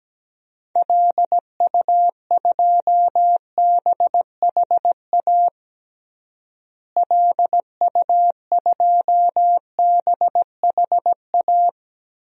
Morse code, heard as a keyed tone, about 17 wpm, 700 Hz. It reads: LU2BHA LU2BHA
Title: DXCC-100 Call Signs - 6 Characters - Repeated 2x - 17wpm